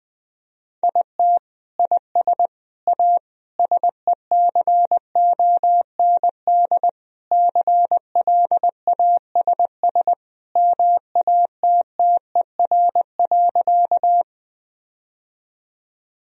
Morse code, heard as a keyed tone, about 20 wpm, 700 Hz. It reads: IT IS A SECOND CLASS MATTER.
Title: Sentences from Top 1000 Words - Review Part 2 - 20wpm